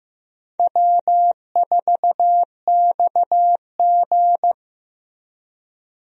Morse code, heard as a keyed tone, about 15 wpm, 700 Hz. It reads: W4XG
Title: USA Call Signs - 4 Characters - Repeated 2x - 15wpm